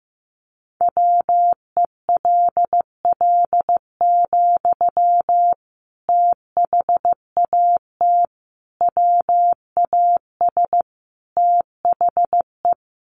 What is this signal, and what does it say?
Morse code, heard as a keyed tone, about 15 wpm, 700 Hz. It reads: WELL, THAT WAS THE
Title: Sentences from Top 1200 Words 15wpm